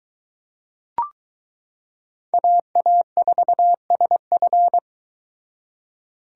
Morse code, heard as a keyed tone, about 22 wpm, 700 Hz. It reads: AA4SF